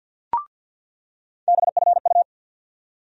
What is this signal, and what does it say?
Morse code, heard as a keyed tone, about 54 wpm, 700 Hz. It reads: BFU
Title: USA Call Sign Postfixes - 3 Letters 50wpm